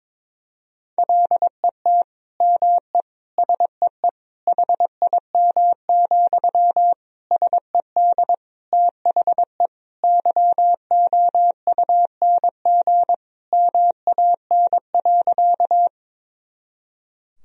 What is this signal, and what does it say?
Morse code, heard as a keyed tone, about 22 wpm, 700 Hz. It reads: LET ME SEE HIM, SED THE YOUNG MAN.
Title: Sentences from Top 400 Words and Abbreviations 22wpm